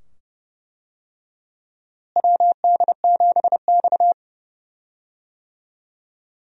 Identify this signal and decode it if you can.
Morse code, heard as a keyed tone, about 32 wpm, 700 Hz. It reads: WD7X
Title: USA Call Signs - 4 Characters - Repeated 2x - 30wpm